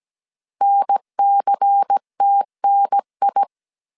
Morse code, heard as a keyed tone, about 17 wpm, 800 Hz. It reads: NCTNI
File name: CW / DET / 1 6 1500C S A 16wpm 2635